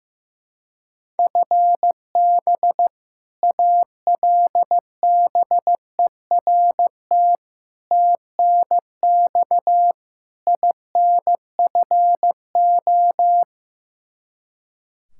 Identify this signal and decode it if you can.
Morse code, heard as a keyed tone, about 15 wpm, 700 Hz. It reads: FB ALBERT TNX INFO
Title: Practice QSO - Exchange 3 - More Info - 15wpm